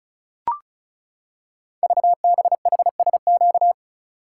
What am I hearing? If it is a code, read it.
Morse code, heard as a keyed tone, about 36 wpm, 700 Hz. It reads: VBHSQ
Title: Code Groups - 5 Letter:Numbers - 35wpm